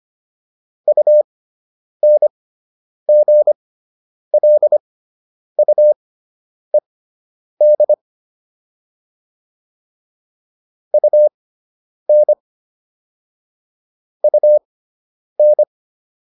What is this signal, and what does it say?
Morse code, heard as a keyed tone, about 25 wpm, spaced as for 9 wpm, 600 Hz. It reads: UNGLUED UN UN